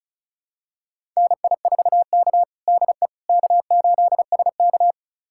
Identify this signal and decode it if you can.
Morse code, heard as a keyed tone, about 36 wpm, 700 Hz. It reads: NI4K DE K8SK
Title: USA Callsign DE Callsign - 4 Characters - 35wpm